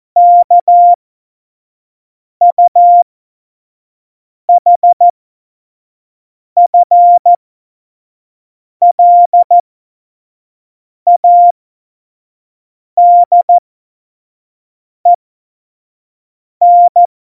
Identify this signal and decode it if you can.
Morse code, heard as a keyed tone, about 14 wpm, spaced as for 5 wpm, 700 Hz. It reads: KUHFLADEN
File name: Morsekurs / Witze / Witz 15. Mai 2018, 25 F70